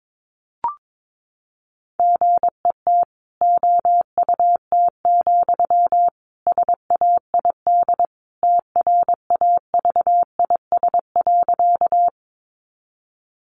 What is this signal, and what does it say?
Morse code, heard as a keyed tone, about 22 wpm, 700 Hz. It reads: GET OUT, SAID TRAVIS.
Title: A Sound Of Thunder 22wpm